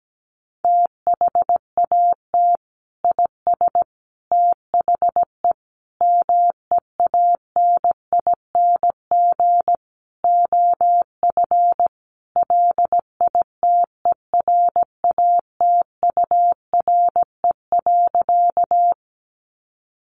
Morse code, heard as a keyed tone, about 17 wpm, 700 Hz. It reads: THAT IS THE MEANING OF LITERATURE.
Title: Sentences from Top 900 Words 17wpm